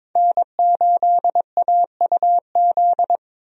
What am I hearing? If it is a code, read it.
Morse code, heard as a keyed tone, about 22 wpm, 700 Hz. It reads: N8AUZ
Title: USA Call Signs - 5 Characters - 22wpm